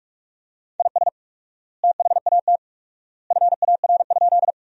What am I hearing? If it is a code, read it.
Morse code, heard as a keyed tone, about 46 wpm, 700 Hz. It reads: IS THAT FAR?
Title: 4x Word-Spacing - Sentences from Top 300 Words - 45wpm